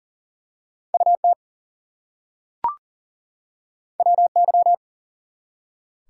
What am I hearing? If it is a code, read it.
Morse code, heard as a keyed tone, about 40 wpm, 700 Hz. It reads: UT WY
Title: US States and Canadian Provinces 40wpm